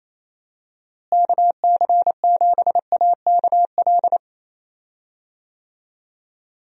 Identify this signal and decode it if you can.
Morse code, heard as a keyed tone, about 28 wpm, 700 Hz. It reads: KC7AKL
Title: USA Call Signs - 6 Characters - Repeated 2x - 28wpm